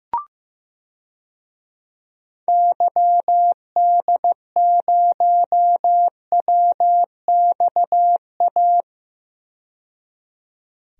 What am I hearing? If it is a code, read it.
Morse code, heard as a keyed tone, about 15 wpm, 700 Hz. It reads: YD0WXA
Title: DXCC-100 Call Signs - 6 Characters - 15wpm